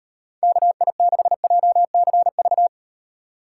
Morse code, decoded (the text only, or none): KI6JCV